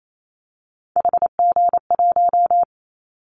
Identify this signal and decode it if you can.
Morse code, heard as a keyed tone, about 28 wpm, 700 Hz. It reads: HG1
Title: Callsign Prefixes for Top 25 DXCC - Repeated 3x - 28wpm